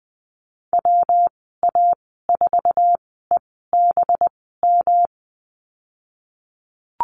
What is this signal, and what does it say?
Morse code, heard as a keyed tone, about 20 wpm, spaced as for 15 wpm, 700 Hz. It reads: WA4EBM
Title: 207 Lesson 33g - Letter j - Call Signs 20wpm